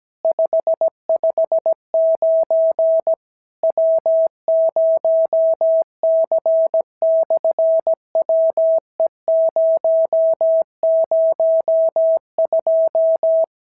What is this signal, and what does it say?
Morse code, heard as a keyed tone, about 17 wpm, 650 Hz. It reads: 559 W0C/WE002
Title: SOTA QSO Practice 17wpm